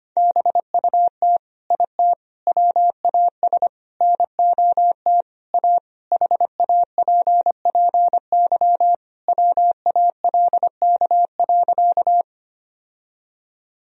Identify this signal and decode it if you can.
Morse code, heard as a keyed tone, about 25 wpm, 700 Hz. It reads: BUT IT WAS NOT A HAPPY WALK.
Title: Sentences from Top 1000 Words - Review Part 1 - 25wpm